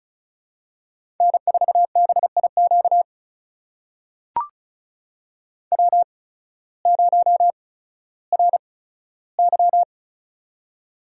Morse code, heard as a keyed tone, about 36 wpm, spaced as for 16 wpm, 700 Hz. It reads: N4BIQ W0RY
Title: USA Call Signs - 8x Character Spacing - 35wpm